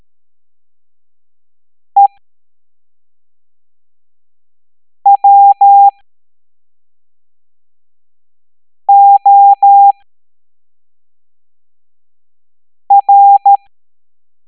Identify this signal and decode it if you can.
Morse code, heard as a keyed tone, about 13 wpm, spaced as for 3 wpm, 800 Hz. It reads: EWOR